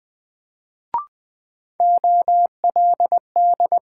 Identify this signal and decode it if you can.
Morse code, heard as a keyed tone, about 20 wpm, 700 Hz. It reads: OLD
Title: Sentences from Top 200 Words - New Word Drill - 20wpm